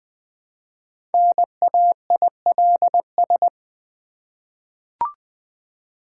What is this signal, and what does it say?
Morse code, heard as a keyed tone, about 20 wpm, 700 Hz. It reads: NAILS
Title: Easy Words with Hints - 20wpm